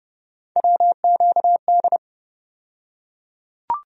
Morse code, heard as a keyed tone, about 29 wpm, 700 Hz. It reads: WQD
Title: USA Call Sign Postfixes - 3 Letters 30wpm